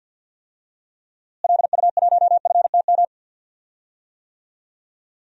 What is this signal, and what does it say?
Morse code, heard as a keyed tone, about 47 wpm, 700 Hz. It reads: LU1FTM
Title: DXCC-100 Call Signs - 6 Characters - Repeated 3x - 50wpm